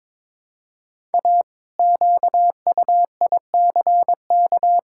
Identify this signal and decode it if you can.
Morse code, heard as a keyed tone, about 22 wpm, 700 Hz. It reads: A QUICK
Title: Pangram Challenge v2 - 22wpm